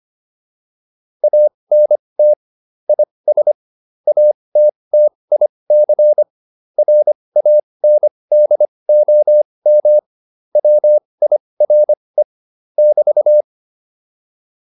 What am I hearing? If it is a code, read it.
Morse code, heard as a keyed tone, about 25 wpm, spaced as for 20 wpm, 600 Hz. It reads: ANT IS ATTIC RANDOM WIRE <BT>